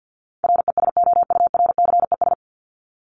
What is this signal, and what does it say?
Morse code, heard as a keyed tone, about 50 wpm, 700 Hz. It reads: RESOURCES
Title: ES:S Suffix - Top 100 Words with Base Words 50wpm